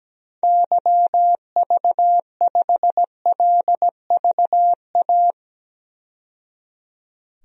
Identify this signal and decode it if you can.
Morse code, heard as a keyed tone, about 17 wpm, 700 Hz. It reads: YV5LVA
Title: DXCC-100 Call Signs - 6 Characters - 17wpm